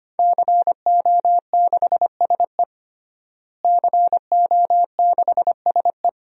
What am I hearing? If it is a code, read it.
Morse code, heard as a keyed tone, about 25 wpm, 700 Hz. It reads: CO6SE CO6SE
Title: DXCC-100 Call Signs - 5 Characters - Repeated 2x - 25wpm